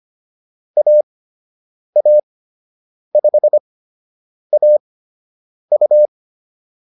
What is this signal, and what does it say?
Morse code, heard as a keyed tone, about 25 wpm, spaced as for 8 wpm, 600 Hz. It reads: AA5AU